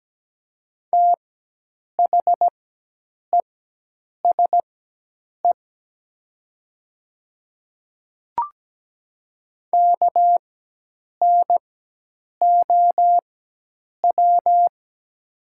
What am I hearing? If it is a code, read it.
Morse code, heard as a keyed tone, about 17 wpm, spaced as for 8 wpm, 700 Hz. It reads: THESE KNOW
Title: 4x Character Spacing - Top 100 Words - 17wpm